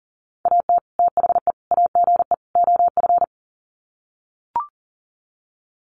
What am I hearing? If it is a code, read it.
Morse code, heard as a keyed tone, about 37 wpm, 700 Hz. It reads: AT THE AGE OF
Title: Sets of 4 Words 40wpm